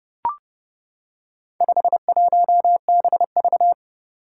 Morse code, heard as a keyed tone, about 29 wpm, 700 Hz. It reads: 51BV
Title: Four Challenging Letter-Number Combinations 30wpm